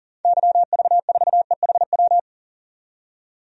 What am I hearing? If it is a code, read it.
Morse code, heard as a keyed tone, about 40 wpm, 700 Hz. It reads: YV4EHW
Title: DXCC-100 Call Signs - 6 Characters - 40wpm